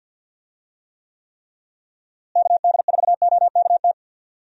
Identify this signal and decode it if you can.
Morse code, heard as a keyed tone, about 52 wpm, 700 Hz. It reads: KD4OKT
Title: USA Call Signs - 6 Characters - Repeated 3x - 50wpm